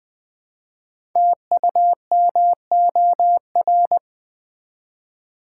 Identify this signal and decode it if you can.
Morse code, heard as a keyed tone, about 20 wpm, 700 Hz. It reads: TUMOR
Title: Build A Word - 1 Letter At A Time - 5-Letter Words - 20wpm